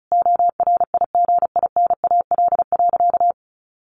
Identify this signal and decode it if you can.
Morse code, heard as a keyed tone, about 34 wpm, 700 Hz. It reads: ORIGINAL.
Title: Sentences from Top 1400 Words 35wpm